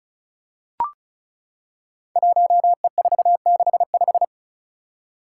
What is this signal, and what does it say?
Morse code, heard as a keyed tone, about 34 wpm, 700 Hz. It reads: 1E465
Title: Code Groups - 5 Letter:Numbers - 35wpm